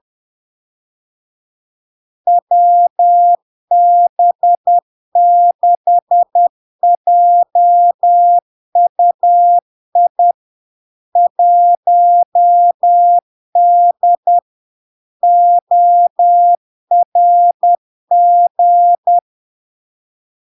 Morse code, heard as a keyed tone, about 10 wpm, 700 Hz. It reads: WB6JUI 1D ORG
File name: LARK-FD-2022-Log-in-CW-10wpm-700-Hz